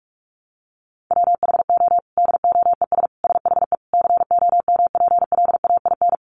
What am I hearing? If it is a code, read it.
Morse code, heard as a keyed tone, about 44 wpm, 700 Hz. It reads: WHO DOES SHE COMPLAIN
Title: Sentences from Top 1300 Words 45wpm